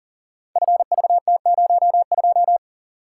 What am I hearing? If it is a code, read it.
Morse code, heard as a keyed tone, about 41 wpm, 700 Hz. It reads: FVT02
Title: Code Groups - 5 Letter:Numbers - 40wpm